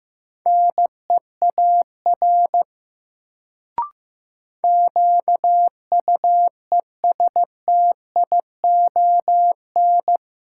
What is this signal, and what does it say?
Morse code, heard as a keyed tone, about 16 wpm, 700 Hz. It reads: NEAR QUESTION
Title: Sets of 5 Words Encore - New Word Drill - 15wpm